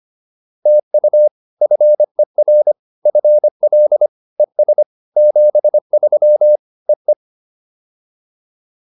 Morse code, heard as a keyed tone, about 25 wpm, 600 Hz. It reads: TU FER FL ES 73 EE